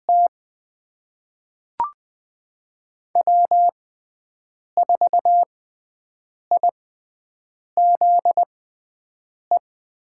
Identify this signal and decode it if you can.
Morse code, heard as a keyed tone, about 20 wpm, spaced as for 7 wpm, 700 Hz. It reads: T W4IZE
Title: USA Call Signs - 5 Characters - 6x Character Spacing - 20wpm